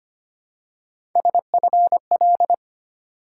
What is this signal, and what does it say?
Morse code, heard as a keyed tone, about 25 wpm, 700 Hz. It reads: SFL
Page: morse-code-ninja-practice-25wpm